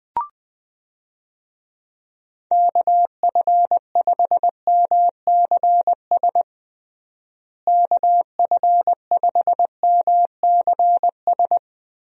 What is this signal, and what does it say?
Morse code, heard as a keyed tone, about 20 wpm, 700 Hz. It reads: KF5MCS KF5MCS
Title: USA Call Signs - 6 Characters - Repeated 2x - 20wpm